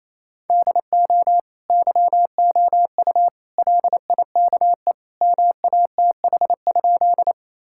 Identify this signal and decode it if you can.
Morse code, heard as a keyed tone, about 28 wpm, 700 Hz. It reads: DO YOU LIKE MATH?
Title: Sentences from Top 1000 Words - Review Part 6 - 28wpm